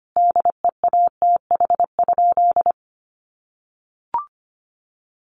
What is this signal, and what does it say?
Morse code, heard as a keyed tone, about 25 wpm, 700 Hz. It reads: DEATH?
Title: Sentences from Top 1100 Words 25wpm